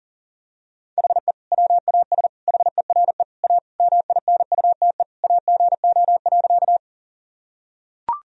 Morse code, heard as a keyed tone, about 40 wpm, 700 Hz. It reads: HE WAS HERE A MINUTE AGO.
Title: Sentences from Top 1000 Words - Review Part 3 - 40wpm